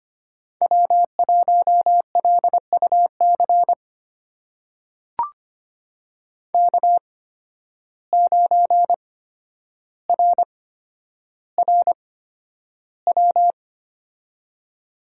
Morse code, heard as a keyed tone, about 25 wpm, spaced as for 11 wpm, 700 Hz. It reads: W1LUC K9RRW
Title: USA Call Signs - 5 Characters - 8x Character Spacing - 25wpm